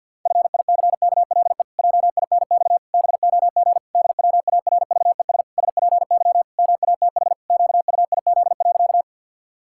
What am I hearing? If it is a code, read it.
Morse code, heard as a keyed tone, about 50 wpm, 700 Hz. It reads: FICKLE JINX BOG DWARVES SPY MATH QUIZ.
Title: Pangram Challenge v2 - 50wpm